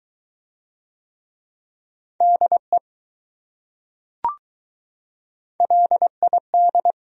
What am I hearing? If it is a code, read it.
Morse code, heard as a keyed tone, about 20 wpm, 700 Hz. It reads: DE LID